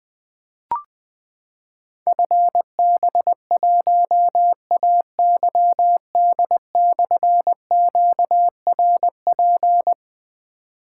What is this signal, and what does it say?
Morse code, heard as a keyed tone, about 20 wpm, 700 Hz. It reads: FB1AYD/QRP
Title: Difficult Call Signs 20wpm